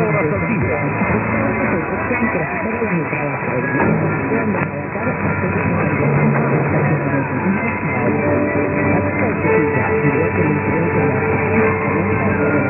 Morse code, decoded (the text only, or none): none